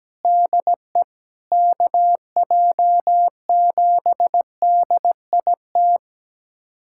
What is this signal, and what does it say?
Morse code, heard as a keyed tone, about 17 wpm, 700 Hz. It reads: DE KJ7DIT
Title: USA Callsign DE Callsign - 6 Characters - 17wpm